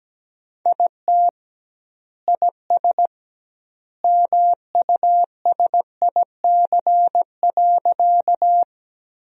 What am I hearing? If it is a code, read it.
Morse code, heard as a keyed tone, about 17 wpm, 700 Hz. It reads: IT IS MUSIC.
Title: 2x Word-Spacing - Sentences from Top 300 Words - 17wpm